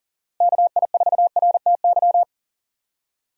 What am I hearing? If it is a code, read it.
Morse code, heard as a keyed tone, about 37 wpm, 700 Hz. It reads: KI4RTY
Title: USA Call Signs - 6 Characters - Repeated 2x - 40wpm